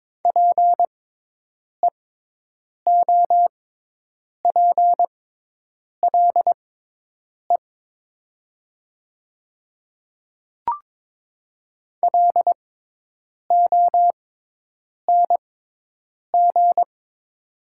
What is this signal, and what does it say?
Morse code, heard as a keyed tone, about 22 wpm, spaced as for 8 wpm, 700 Hz. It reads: PEOPLE LONG
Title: 6x Character Spacing - Top 500 Words - 22wpm